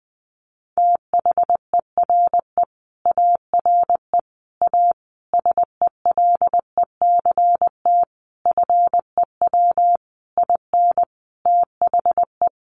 Morse code, heard as a keyed tone, about 20 wpm, 700 Hz. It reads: THERE ARE A SELECT FEW IN THE
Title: Sentences from Top 1300 Words 20wpm